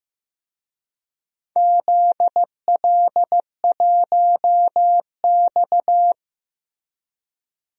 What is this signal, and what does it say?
Morse code, heard as a keyed tone, about 15 wpm, 700 Hz. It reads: ZL1X
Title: DXCC-100 Call Signs - 4 Characters - Repeated 2x - 15wpm